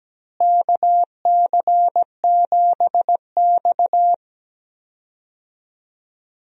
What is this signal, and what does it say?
Morse code, heard as a keyed tone, about 17 wpm, 700 Hz. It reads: KC7X
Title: USA Call Signs - 4 Characters - Repeated 3x - 17wpm